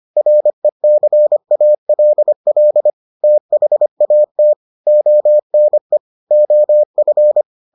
Morse code, heard as a keyed tone, about 25 wpm, 600 Hz. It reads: RECALL THAT ONE OF